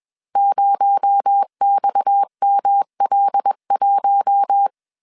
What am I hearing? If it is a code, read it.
Morse code, heard as a keyed tone, about 21 wpm, 800 Hz. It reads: 0XML1
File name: CW / DET / 1 6 1500C S C 20wpm 2116